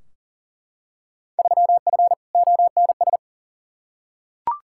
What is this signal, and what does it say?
Morse code, heard as a keyed tone, about 40 wpm, 700 Hz. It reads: 3F ONS
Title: Field Day Exchanges - 40wpm